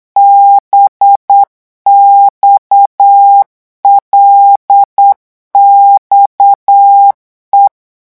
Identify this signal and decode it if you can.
Morse code, heard as a keyed tone, about 8 wpm, 800 Hz. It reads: BXLXE
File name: morse mp3 / revision8-vitesse 8 mots